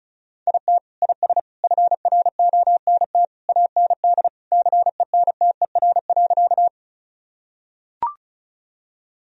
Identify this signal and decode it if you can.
Morse code, heard as a keyed tone, about 36 wpm, 700 Hz. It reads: IT IS FRONT AND CENTER.
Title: Sentences from Top 1000 Words - Review Part 2 - 35wpm